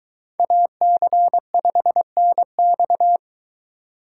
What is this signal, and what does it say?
Morse code, heard as a keyed tone, about 23 wpm, 700 Hz. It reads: AC5NX